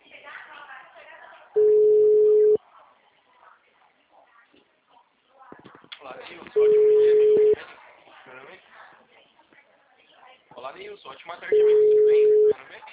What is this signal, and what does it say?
Ringback tone (Europe, 425 Hz), with speech.